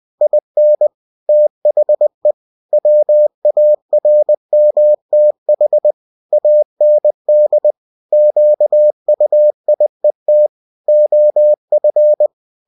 Morse code, heard as a keyed tone, about 20 wpm, 600 Hz. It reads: IN THE WARMTH AND QUIET OF
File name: cw-books / winnie-the-pooh / winnie-20wpm-page-0101